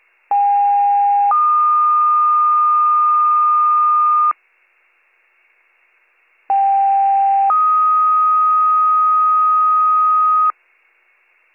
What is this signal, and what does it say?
Two-tone page, heard as a one-second tone, then a three-second tone, about 810 then 1220 Hz, about 790 then 1250 Hz. It is heard from 0.3 to 10.5 s.